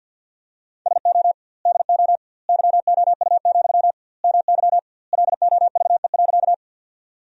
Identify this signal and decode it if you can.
Morse code, heard as a keyed tone, about 50 wpm, 700 Hz. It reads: SO DO YOU, MY LOVE.
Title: 2x Word-Spacing - Sentences from Top 400 Words - 50wpm